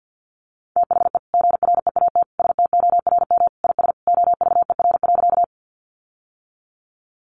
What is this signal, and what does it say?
Morse code, heard as a keyed tone, about 50 wpm, 700 Hz. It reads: THE GREAT STORM IS OVER.
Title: Sentences from Top 1500 Words 50wpm